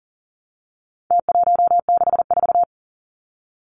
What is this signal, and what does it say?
Morse code, heard as a keyed tone, about 40 wpm, 700 Hz. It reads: T164
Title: Numbers - Contest Style - 4-Digit 40wpm